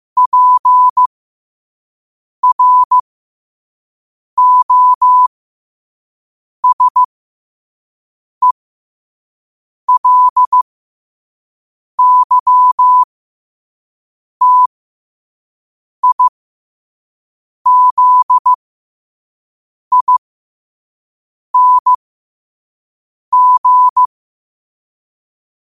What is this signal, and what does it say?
Morse code, heard as a keyed tone, about 15 wpm, spaced as for 5 wpm, 1000 Hz. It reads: PROSELYTIZING
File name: qotdfiles / 5wpm